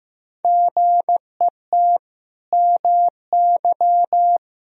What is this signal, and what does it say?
Morse code, heard as a keyed tone, about 15 wpm, 700 Hz. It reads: GET MY
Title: Sentences from Top 1000 Words - Review Part 2 - 15wpm